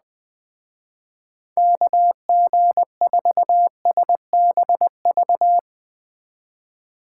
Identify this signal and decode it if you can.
Morse code, heard as a keyed tone, about 21 wpm, 700 Hz. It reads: KG4SBV